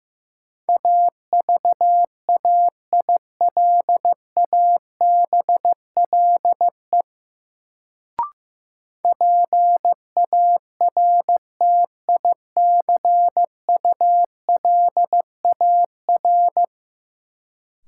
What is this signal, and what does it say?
Morse code, heard as a keyed tone, about 15 wpm, 700 Hz. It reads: AVAILABLE PARTICULAR
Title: Sentences from Top 600 Words - New Word Drill - 15wpm